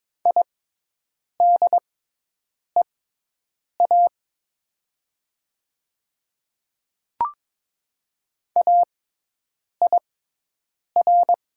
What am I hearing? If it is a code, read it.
Morse code, heard as a keyed tone, about 22 wpm, spaced as for 8 wpm, 700 Hz. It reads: IDEA AIR
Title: 6x Character Spacing - Top 300 Words - 22wpm